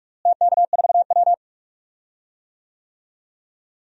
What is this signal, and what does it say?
Morse code, heard as a keyed tone, about 46 wpm, 700 Hz. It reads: TK4W
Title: DXCC-100 Call Signs - 4 Characters - Repeated 3x - 45wpm